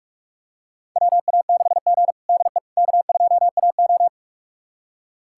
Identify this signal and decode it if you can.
Morse code, heard as a keyed tone, about 41 wpm, 700 Hz. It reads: WA6G DE K2AO
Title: USA Callsign DE Callsign - 4 Characters - 45wpm